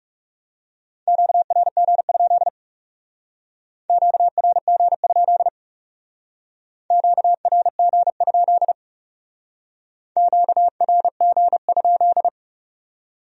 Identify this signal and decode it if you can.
Morse code, heard as a keyed tone, about 39 wpm, 700 Hz. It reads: QRG? QRG? QRG? QRG?